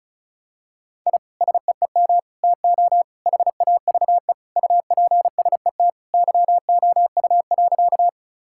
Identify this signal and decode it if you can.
Morse code, heard as a keyed tone, about 34 wpm, 700 Hz. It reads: I SEEM TO HAVE UPSET YOU.
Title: Sentences from Top 1000 Words - Review Part 5 - 35wpm